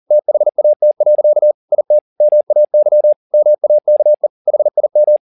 Morse code, heard as a keyed tone, about 40 wpm, 600 Hz. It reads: THAT. IT MAY MAKE HIM
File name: the old man and the sea 2 40wpm